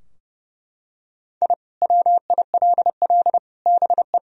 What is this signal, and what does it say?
Morse code, heard as a keyed tone, about 30 wpm, 700 Hz. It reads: I WILL BE
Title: Sentences from Top 1000 Words - Review Part 3 - 30wpm